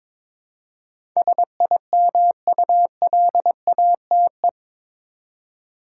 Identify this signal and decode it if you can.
Morse code, heard as a keyed tone, about 21 wpm, 700 Hz. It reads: SIMULATE
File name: practice / 22wpm / Eight Letter Words - 22wpm